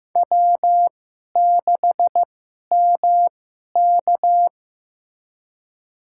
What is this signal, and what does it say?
Morse code, heard as a keyed tone, about 15 wpm, spaced as for 11 wpm, 700 Hz. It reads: W6MK
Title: USA Call Signs - 4 Characters - 2x Character Spacing - 15wpm